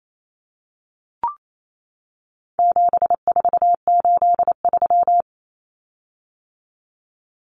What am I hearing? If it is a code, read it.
Morse code, heard as a keyed tone, about 28 wpm, 700 Hz. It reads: 7483
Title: Common Cut Numbers - 4-Digit 28wpm